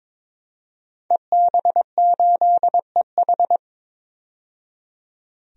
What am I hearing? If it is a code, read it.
Morse code, heard as a keyed tone, about 21 wpm, 700 Hz. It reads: EB8EH
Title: DXCC-100 Call Signs - 5 Characters - 22wpm